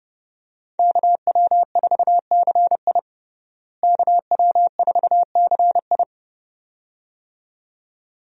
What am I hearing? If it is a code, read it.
Morse code, heard as a keyed tone, about 30 wpm, 700 Hz. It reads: KW4CI KW4CI
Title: USA Call Signs - 5 Characters - Repeated 2x - 30wpm